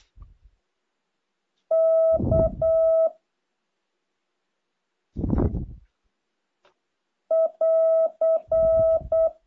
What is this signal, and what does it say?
Morse code, heard as a keyed tone, about 8 wpm, 650 Hz. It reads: K <AR>
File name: jh8cbh / 2023cw